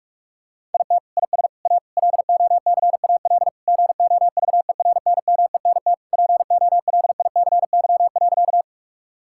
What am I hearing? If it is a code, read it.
Morse code, heard as a keyed tone, about 42 wpm, 700 Hz. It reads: IT IS A LOCAL GOVERNMENT POLICY.
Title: Sentences from Top 1000 Words - Review Part 5 - 45wpm